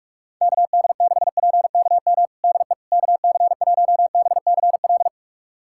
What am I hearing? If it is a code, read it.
Morse code, heard as a keyed tone, about 46 wpm, 700 Hz. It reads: KD6PKM DE KC1BCL
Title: USA Callsign DE Callsign 45wpm